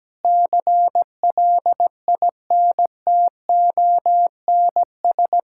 Morse code, heard as a keyed tone, about 17 wpm, 700 Hz. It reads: CLINTONS
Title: Pangram Challenge v2 - 17wpm